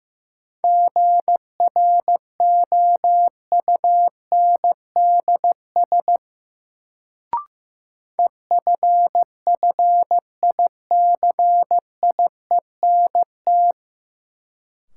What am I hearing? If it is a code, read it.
Morse code, heard as a keyed tone, about 15 wpm, 700 Hz. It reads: GROUNDS EFFICIENT
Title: Sentences from Top 1300 Words - New Word Drill - 15wpm